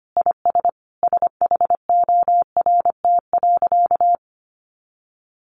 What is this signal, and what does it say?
Morse code, heard as a keyed tone, about 25 wpm, 700 Hz. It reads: IS SHORT.
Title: Sentences from Top 500 Words 25wpm